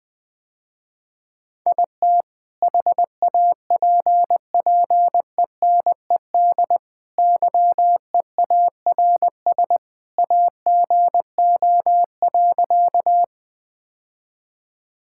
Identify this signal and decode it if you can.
Morse code, heard as a keyed tone, about 20 wpm, 700 Hz. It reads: IT HAPPENED YEARS AGO.
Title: Sentences with 2-Word Phrases - Spoken Beginning - 20wpm